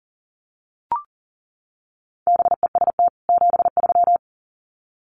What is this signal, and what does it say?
Morse code, heard as a keyed tone, about 40 wpm, 700 Hz. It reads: BEST 73
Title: Common QSO Phrases 40wpm